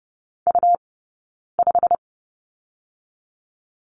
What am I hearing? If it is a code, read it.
Morse code, heard as a keyed tone, about 30 wpm, 700 Hz. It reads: U 5
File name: Lesson 17d - Letter u - Two Characters 30wpm